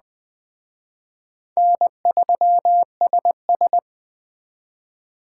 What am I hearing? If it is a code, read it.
Morse code, heard as a keyed tone, about 23 wpm, 700 Hz. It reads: N3SS
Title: Speed Racing - USA 1x2 Call Signs 20wpm